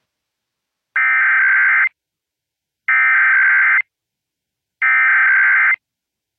Emergency Alert System (SAME) tones (header bursts), heard as harsh buzzing bursts. It is heard from 1.0 to 5.7 s.